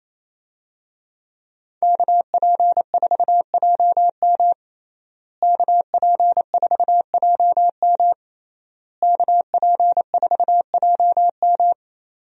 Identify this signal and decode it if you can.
Morse code, heard as a keyed tone, about 27 wpm, 700 Hz. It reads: KP4JM KP4JM KP4JM
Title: DXCC-100 Call Signs - 5 Characters - Repeated 3x - 28wpm